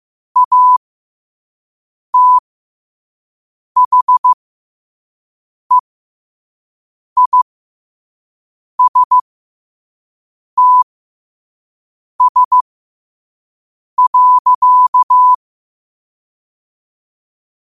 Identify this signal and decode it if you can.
Morse code, heard as a keyed tone, about 15 wpm, spaced as for 5 wpm, 1000 Hz. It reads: ATHEISTS.
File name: qotdfiles / 5wpm